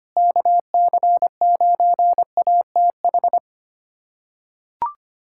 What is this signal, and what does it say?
Morse code, heard as a keyed tone, about 25 wpm, 700 Hz. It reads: KC9ATH
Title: USA Callsign DE Callsign - 6 Characters - 25wpm